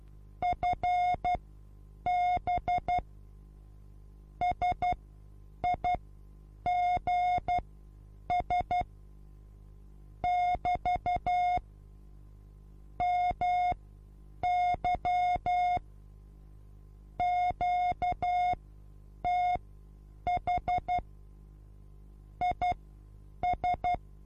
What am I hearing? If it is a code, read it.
Morse code, heard as a keyed tone, about 12 wpm, spaced as for 8 wpm, 720 Hz. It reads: FB SIGS <BT> MY QTH IS
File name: repos / Morsekurs / DJ4UF Morsen23